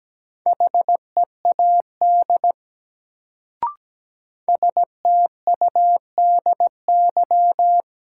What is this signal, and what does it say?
Morse code, heard as a keyed tone, about 17 wpm, 700 Hz. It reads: HEAD STUDY
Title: Sentences from Top 300 Words - New Word Drill - 17wpm